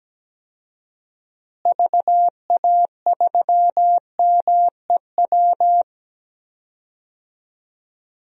Morse code, heard as a keyed tone, about 17 wpm, 700 Hz. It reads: VA3MEW
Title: DXCC-100 Call Signs - 6 Characters - Repeated 3x - 17wpm